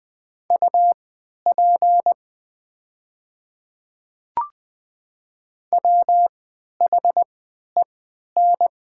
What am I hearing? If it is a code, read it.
Morse code, heard as a keyed tone, about 20 wpm, spaced as for 11 wpm, 700 Hz. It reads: UP WHEN
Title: 3x Character Spacing - Top 100 Words - 20wpm